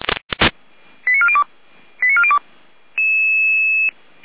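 Five-tone selective call (ZVEI), heard as quick runs of five tones, tones of about 70 ms. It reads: 89402 89402
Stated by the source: ZVEI five-tone call received by SDR.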